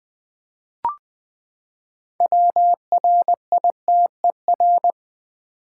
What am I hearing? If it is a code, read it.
Morse code, heard as a keyed tone, about 20 wpm, 700 Hz. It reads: WRITER
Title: Six Letter Words - 20wpm